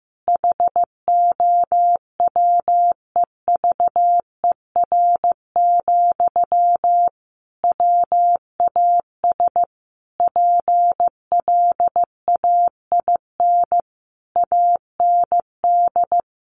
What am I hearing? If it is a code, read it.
Morse code, heard as a keyed tone, about 14 wpm, 700 Hz. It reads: HOWEVER, WAS PLAIN AND
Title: Sentences from Top 1400 Words 15wpm